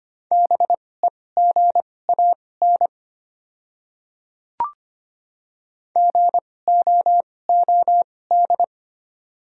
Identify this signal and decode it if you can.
Morse code, heard as a keyed tone, about 25 wpm, spaced as for 18 wpm, 690 Hz. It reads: BEGAN GOOD
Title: 2x Character Spacing - Top 300 Words - 25wpm